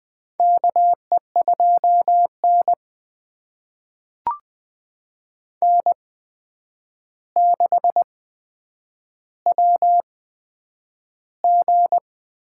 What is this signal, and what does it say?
Morse code, heard as a keyed tone, about 20 wpm, spaced as for 9 wpm, 700 Hz. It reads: KE2N N6WG
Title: USA Call Signs - 4 Characters - 8x Character Spacing - 20wpm